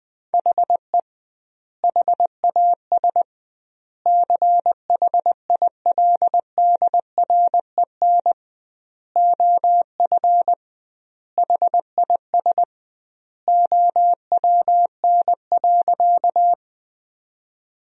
Morse code, heard as a keyed tone, about 21 wpm, 700 Hz. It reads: HE HAS CHILDREN OF HIS OWN.
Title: 2x Word-Spacing - Sentences from Top 300 Words - 20wpm